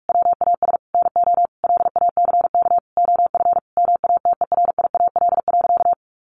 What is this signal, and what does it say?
Morse code, heard as a keyed tone, about 44 wpm, 700 Hz. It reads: WAS NO LACK OF MATERIAL.
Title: Sentences from Top 1100 Words 45wpm